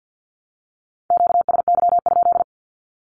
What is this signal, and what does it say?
Morse code, heard as a keyed tone, about 50 wpm, 700 Hz. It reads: QSY?
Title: Q-Codes Repeated 3x 50wpm